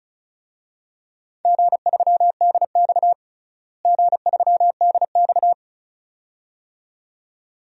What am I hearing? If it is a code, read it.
Morse code, heard as a keyed tone, about 35 wpm, 690 Hz. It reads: G3DX G3DX
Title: DXCC-100 Call Signs - 4 Characters - Repeated 2x - 35wpm